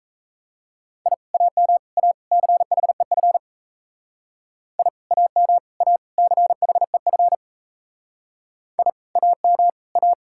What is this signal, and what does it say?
Morse code, heard as a keyed tone, about 34 wpm, 700 Hz. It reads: I AM A CHEF I AM A CHEF I AM A
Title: Speed-Racing Common QSO Phrases - 28wpm